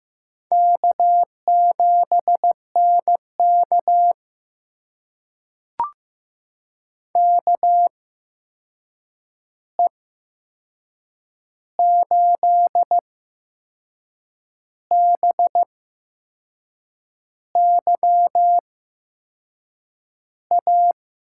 Morse code, heard as a keyed tone, about 15 wpm, spaced as for 6 wpm, 700 Hz. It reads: K7NK KE8BYA
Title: USA Call Signs - 8x Character Spacing - 15wpm